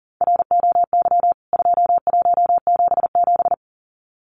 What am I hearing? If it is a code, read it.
Morse code, heard as a keyed tone, about 40 wpm, 700 Hz. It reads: ROY 2177
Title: CWT Exchange 40wpm